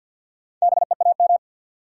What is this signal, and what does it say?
Morse code, heard as a keyed tone, about 54 wpm, 700 Hz. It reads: BEAM
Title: Top 90 QSO Elements 50wpm